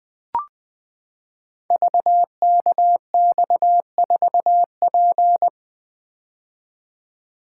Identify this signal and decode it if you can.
Morse code, heard as a keyed tone, about 20 wpm, 700 Hz. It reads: VKX4P
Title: Code Groups - 5 Letter:Numbers - 20wpm